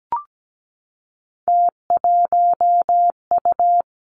Morse code, heard as a keyed tone, about 17 wpm, 700 Hz. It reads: T1U
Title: Three Challenging Letter-Number Combinations 17wpm